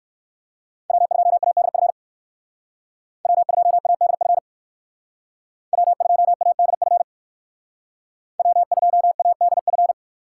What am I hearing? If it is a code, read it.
Morse code, heard as a keyed tone, about 64 wpm, 700 Hz. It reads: W2ADF W2ADF W2ADF W2ADF